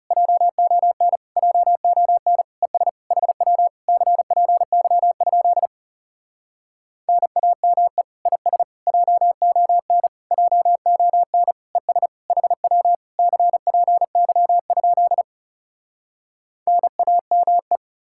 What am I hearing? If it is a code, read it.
Morse code, heard as a keyed tone, about 37 wpm, 700 Hz. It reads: JON JON ES HW CPY? NAME IS JON JON ES HW CPY? NAME
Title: Speed-Racing Common QSO Phrases Encore - 30wpm